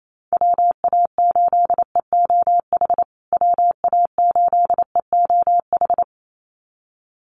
Morse code, heard as a keyed tone, about 28 wpm, 700 Hz. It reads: WA8EOH WA8EOH
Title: USA Call Signs Repeated 2x 28wpm